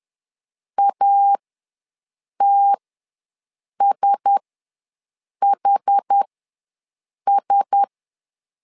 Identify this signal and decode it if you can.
Morse code, heard as a keyed tone, about 10 wpm, spaced as for 6 wpm, 800 Hz. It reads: ATSHS
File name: LEZIONI / 4 Q1S ws 10wpm 2208